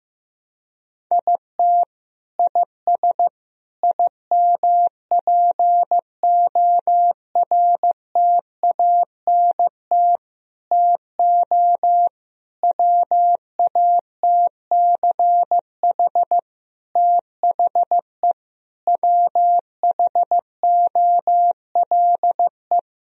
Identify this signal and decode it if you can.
Morse code, heard as a keyed tone, about 16 wpm, 700 Hz. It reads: IT IS IMPORTANT TO WATCH THE WHOLE
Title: Sentences from Top 1000 Words - Review Part 4 - 15wpm